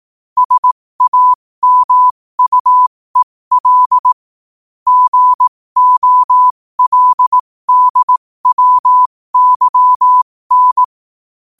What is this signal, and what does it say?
Morse code, heard as a keyed tone, about 18 wpm, spaced as for 15 wpm, 1000 Hz. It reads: SAMUEL GOLDWYN
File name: qotdfiles / 15wpm